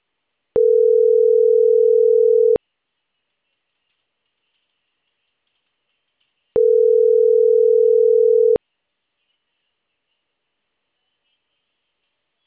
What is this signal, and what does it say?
Ringback tone (North America), 440 + 480 Hz.